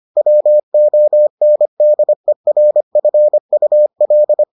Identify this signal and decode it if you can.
Morse code, heard as a keyed tone, about 25 wpm, 600 Hz. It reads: WONDERFUL